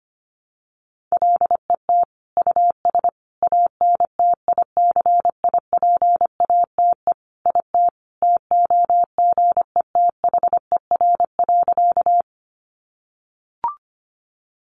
Morse code, heard as a keyed tone, about 25 wpm, 700 Hz. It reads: LET US ANTICIPATE IT TOGETHER.
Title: Sentences from Top 1500 Words 25wpm